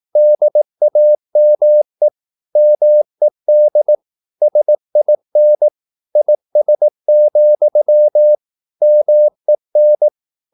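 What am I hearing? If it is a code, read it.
Morse code, heard as a keyed tone, about 18 wpm, 600 Hz. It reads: DAME MED SIN IS, MEN
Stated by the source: Slow (QRS) Morse practice at 18 wpm, CW med Gnister.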